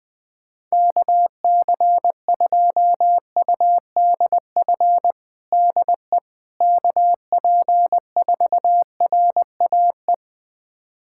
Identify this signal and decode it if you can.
Morse code, heard as a keyed tone, about 20 wpm, 700 Hz. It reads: KC2UDF DE KP4RAE
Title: USA Callsign DE Callsign - 6 Characters - 20wpm